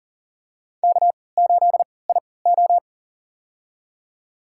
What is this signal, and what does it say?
Morse code, heard as a keyed tone, about 40 wpm, spaced as for 23 wpm, 700 Hz. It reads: K8IO
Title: USA Call Signs - 4 Characters - 3x Character Spacing - 40wpm